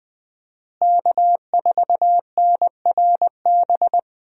Morse code, heard as a keyed tone, about 21 wpm, 700 Hz. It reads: K4NRB
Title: USA Call Signs - 5 Characters - Repeated 3x - 20wpm